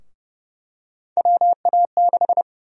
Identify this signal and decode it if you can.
Morse code, heard as a keyed tone, about 30 wpm, 700 Hz. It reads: WA6
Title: USA Call Sign Prefixes - 30wpm